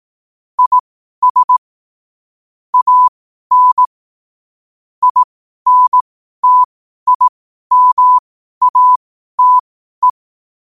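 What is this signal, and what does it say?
Morse code, heard as a keyed tone, about 18 wpm, spaced as for 12 wpm, 1000 Hz. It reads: IS AN INTIMATE
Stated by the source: Morse code quote practice at 12 words per minute.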